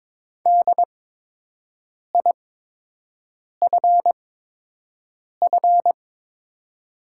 Morse code, heard as a keyed tone, about 23 wpm, spaced as for 6 wpm, 700 Hz. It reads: DIFF
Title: 8x Character Spacing - Top 200 Words - 22wpm